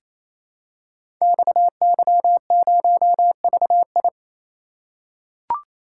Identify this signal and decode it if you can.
Morse code, heard as a keyed tone, about 29 wpm, 700 Hz. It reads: XY0VI
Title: Code Groups - 5 Letter:Numbers - 28wpm